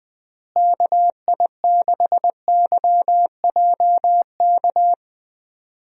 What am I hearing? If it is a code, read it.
Morse code, heard as a keyed tone, about 20 wpm, 700 Hz. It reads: KI6YJK
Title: USA Call Signs - 6 Characters - Repeated 3x - 20wpm